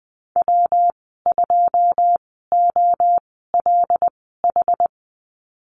Morse code, heard as a keyed tone, about 20 wpm, spaced as for 15 wpm, 700 Hz. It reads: W2OLH
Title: 147 Lesson 24g - Letter y - Call Signs 20wpm